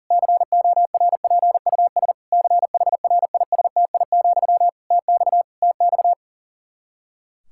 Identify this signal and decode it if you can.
Morse code, heard as a keyed tone, about 40 wpm, 700 Hz. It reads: CORPUS CHRISTI, TX TX
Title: K1USN SST QSO Practice 40wpm